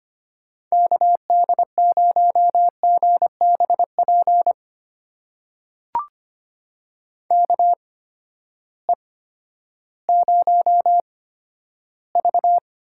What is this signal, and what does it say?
Morse code, heard as a keyed tone, about 25 wpm, spaced as for 12 wpm, 700 Hz. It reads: KD0GBP KE0V
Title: USA Call Signs - 6 Characters - 8x Character Spacing - 25wpm